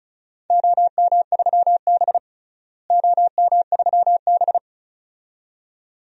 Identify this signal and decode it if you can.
Morse code, heard as a keyed tone, about 36 wpm, 700 Hz. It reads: OM3B OM3B
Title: DXCC-100 Call Signs - 4 Characters - Repeated 2x - 35wpm